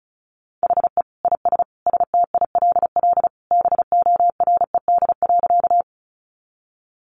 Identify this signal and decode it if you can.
Morse code, heard as a keyed tone, about 34 wpm, 700 Hz. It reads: HE IS STILL BORED.
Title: Easy Phrases 35wpm